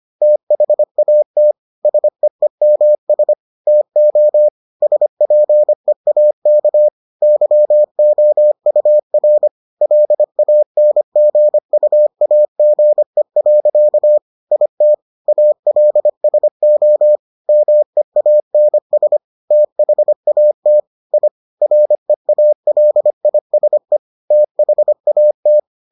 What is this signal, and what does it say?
Morse code, heard as a keyed tone, about 25 wpm, 600 Hz. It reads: THAT SEEMS TO SPEAK YOUR LANGUAGE. IT ALSO MEANS THAT I REALISE THAT